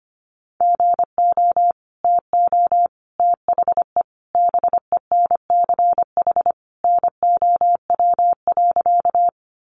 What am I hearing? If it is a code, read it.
Morse code, heard as a keyed tone, about 25 wpm, 700 Hz. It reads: GO TO THE BENCH NOW.
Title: Sentences from Top 1400 Words 25wpm